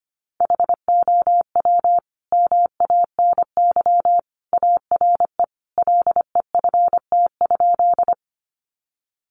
Morse code, heard as a keyed tone, about 25 wpm, 700 Hz. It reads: HOW MANY ARE LEFT?
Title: Sentences from Top 300 Words 25wpm